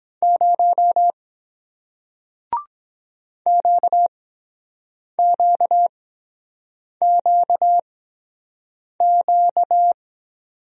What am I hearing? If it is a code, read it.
Morse code, heard as a keyed tone, about 22 wpm, spaced as for 7 wpm, 700 Hz. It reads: 0 QQQQ